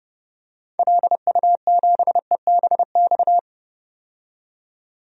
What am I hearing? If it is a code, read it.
Morse code, heard as a keyed tone, about 30 wpm, 700 Hz. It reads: LU7EBX
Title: DXCC-100 Call Signs - 6 Characters - 30wpm